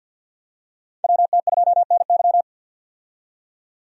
Morse code, heard as a keyed tone, about 50 wpm, 700 Hz. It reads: WT2NY
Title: USA Call Signs - 5 Characters - 50wpm